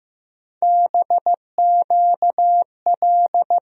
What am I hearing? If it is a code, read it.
Morse code, heard as a keyed tone, about 15 wpm, 700 Hz. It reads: BQL
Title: USA Call Sign Postfixes - 3 Letters 15wpm